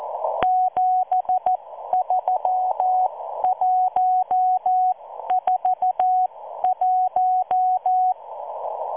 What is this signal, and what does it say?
Morse code, heard as a keyed tone, about 14 wpm, spaced as for 12 wpm, 740 Hz. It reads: 73141